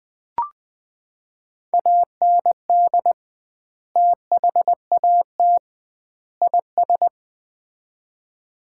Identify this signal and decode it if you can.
Morse code, heard as a keyed tone, about 20 wpm, 700 Hz. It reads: AND THAT IS
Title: 2x Word-Spacing - Sets of 3 Words - 20wpm